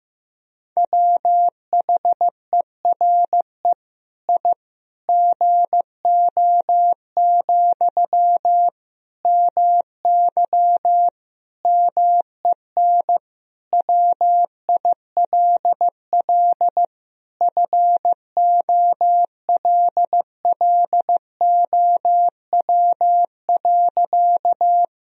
Morse code, heard as a keyed tone, about 15 wpm, 700 Hz. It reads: WHERE I GO, MY MEN WILL FOLLOW.
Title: Sentences from Top 1000 Words - Review Part 1 - 15wpm